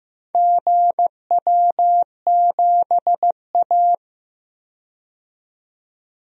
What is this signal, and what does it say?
Morse code, heard as a keyed tone, about 15 wpm, 700 Hz. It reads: GW7A
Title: DXCC-100 Call Signs - 4 Characters - 15wpm